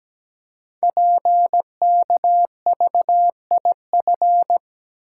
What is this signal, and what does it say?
Morse code, heard as a keyed tone, about 17 wpm, 700 Hz. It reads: PKVIF
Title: Code Groups - 5 Letters - 17wpm